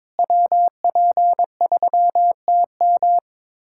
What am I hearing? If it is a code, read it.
Morse code, heard as a keyed tone, about 22 wpm, 700 Hz. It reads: WP3TM
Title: USA Callsign DE Callsign - 5 Characters - 22wpm